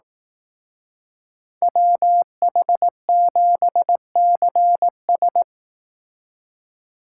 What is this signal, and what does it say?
Morse code, heard as a keyed tone, about 18 wpm, 700 Hz. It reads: WH7CS
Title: Speed Racing - USA 2x2 Call Signs 15wpm